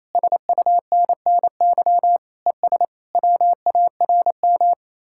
Morse code, heard as a keyed tone, about 34 wpm, 700 Hz. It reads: SUNNY ES WARM
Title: Speed-Racing Common QSO Phrases - 28wpm